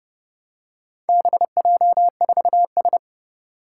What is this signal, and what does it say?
Morse code, heard as a keyed tone, about 32 wpm, 700 Hz. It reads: BJ4S
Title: Four Challenging Letter-Number Combinations 30wpm